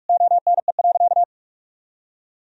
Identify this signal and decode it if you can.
Morse code, heard as a keyed tone, about 46 wpm, 700 Hz. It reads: ONE.
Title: Sentences from Top 1000 Words - Review Part 2 - 45wpm